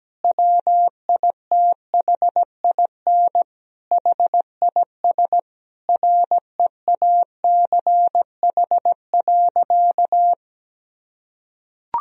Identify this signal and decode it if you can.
Morse code, heard as a keyed tone, about 17 wpm, 700 Hz. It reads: WITHIN HIS REACH.
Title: Sentences from Top 1000 Words - Review Part 4 - 17wpm